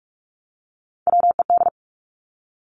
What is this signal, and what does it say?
Morse code, heard as a keyed tone, about 46 wpm, 700 Hz. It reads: WED
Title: Three Challenging Letter-Number Combinations 45wpm